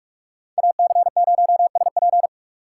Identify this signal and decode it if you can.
Morse code, heard as a keyed tone, about 42 wpm, 700 Hz. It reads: AC0SP
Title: USA Call Signs - 6x Character Spacing - 45wpm